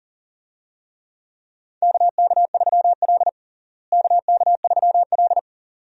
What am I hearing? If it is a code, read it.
Morse code, heard as a keyed tone, about 40 wpm, 700 Hz. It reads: KK3L KK3L
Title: USA Call Signs - 4 Characters - Repeated 2x - 40wpm